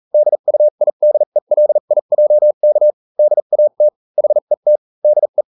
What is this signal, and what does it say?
Morse code, heard as a keyed tone, about 40 wpm, spaced as for 35 wpm, 600 Hz. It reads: DUIDELIJK DAT HET DE